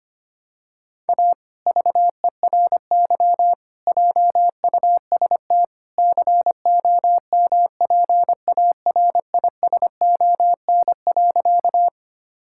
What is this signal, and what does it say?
Morse code, heard as a keyed tone, about 25 wpm, 700 Hz. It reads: A VERY JUST COMPARISON.
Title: Sentences from Top 1600 Words 25wpm